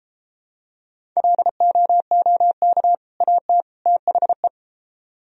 Morse code, heard as a keyed tone, about 31 wpm, 700 Hz. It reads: LOOK AT THE